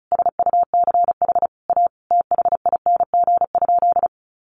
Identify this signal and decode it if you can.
Morse code, heard as a keyed tone, about 34 wpm, 700 Hz. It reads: SUCH A THING?
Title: Sentences from Top 1400 Words 35wpm